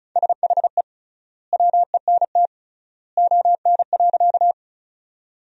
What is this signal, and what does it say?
Morse code, heard as a keyed tone, about 36 wpm, 700 Hz. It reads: SHE WENT ON.
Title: 3x Word-Spacing - Sentences from Top 200 Words - 35wpm